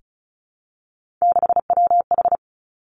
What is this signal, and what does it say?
Morse code, heard as a keyed tone, about 36 wpm, 700 Hz. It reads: 6WH
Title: Three Challenging Letter-Number Combinations 35wpm